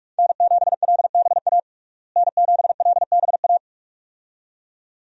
Morse code, heard as a keyed tone, about 46 wpm, 700 Hz. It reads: N7LBA N7LBA
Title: USA Call Signs - 5 Characters - Repeated 2x - 45wpm